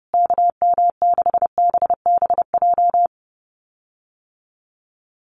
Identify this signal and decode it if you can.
Morse code, heard as a keyed tone, about 30 wpm, 700 Hz. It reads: KM6BBJ
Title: USA Call Signs Repeated 3x 30wpm